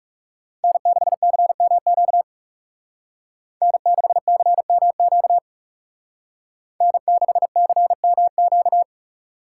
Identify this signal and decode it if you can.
Morse code, heard as a keyed tone, about 35 wpm, 700 Hz. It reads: N6CMQ N6CMQ N6CMQ
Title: Speed Racing - USA 1x3 Call Signs 30wpm